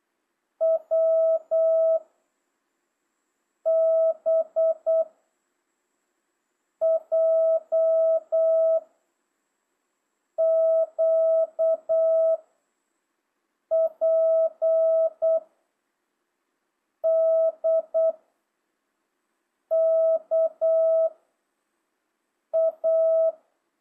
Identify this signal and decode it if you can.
Morse code, heard as a keyed tone, about 8 wpm, 660 Hz. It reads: W B J Q P D K A
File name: jh8cbh / 2025cw